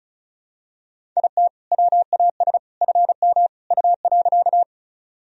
Morse code, heard as a keyed tone, about 36 wpm, 700 Hz. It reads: IT WAS FM U.
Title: Sentences from Top 100 Words and Abbreviations 35wpm